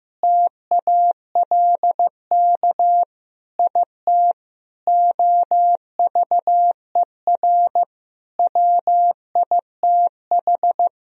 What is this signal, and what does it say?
Morse code, heard as a keyed tone, about 16 wpm, 700 Hz. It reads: TALK IT OVER WITH
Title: Sentences from Top 1000 Words - Review Part 2 - 15wpm